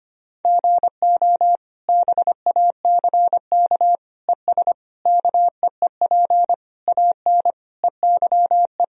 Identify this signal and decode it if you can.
Morse code, heard as a keyed tone, about 25 wpm, 700 Hz. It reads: GO BACK ES KEEP AN EYE
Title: Sentences from Top 300 Words and Abbreviations 25wpm